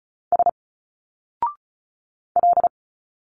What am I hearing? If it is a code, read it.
Morse code, heard as a keyed tone, about 34 wpm, 700 Hz. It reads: S L
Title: Single Letters 35wpm